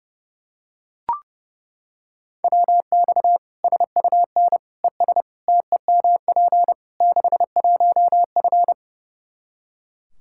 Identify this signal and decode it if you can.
Morse code, heard as a keyed tone, about 32 wpm, 700 Hz. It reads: WX SUN ES TEMP 61F
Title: Practice QSO - Exchange 2 - Info - 30wpm